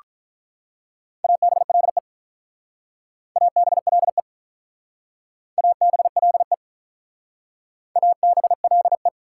Morse code, heard as a keyed tone, about 47 wpm, 700 Hz. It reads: ABLE ABLE ABLE ABLE